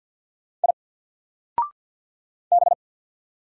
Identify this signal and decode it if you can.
Morse code, heard as a keyed tone, about 52 wpm, 700 Hz. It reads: I B
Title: Single Letter-Number 50wpm